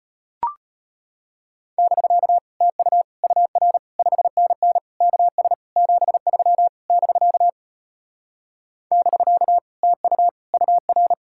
Morse code, heard as a keyed tone, about 33 wpm, 700 Hz. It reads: <BK> TU UR 5NN KS 73 <BK> <BK> TU UR